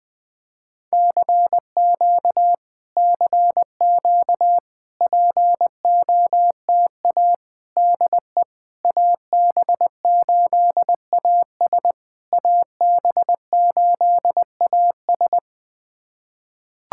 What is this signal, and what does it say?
Morse code, heard as a keyed tone, about 20 wpm, 700 Hz. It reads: CQ CQ POTA DE AB8AS AB8AS
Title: POTA QSO Practice - 20wpm